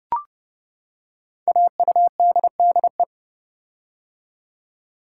Morse code, heard as a keyed tone, about 29 wpm, 700 Hz. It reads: AUDDE
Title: Code Groups - 5 Letters - 30wpm